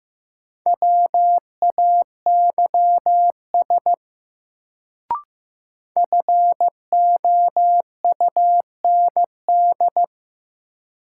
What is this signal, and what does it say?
Morse code, heard as a keyed tone, about 15 wpm, 700 Hz. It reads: WAYS FOUND
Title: Sets of 5 Words Encore - New Word Drill - 15wpm